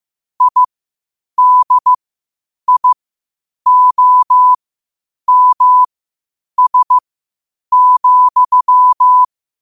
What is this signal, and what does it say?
Morse code, heard as a keyed tone, about 15 wpm, spaced as for 8 wpm, 1000 Hz. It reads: IDIOMS,
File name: qotdfiles / 8wpm